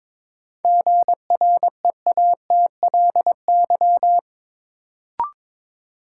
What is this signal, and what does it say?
Morse code, heard as a keyed tone, about 22 wpm, 700 Hz. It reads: GREATLY
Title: Sentences from Top 1400 Words - New Word Drill - 22wpm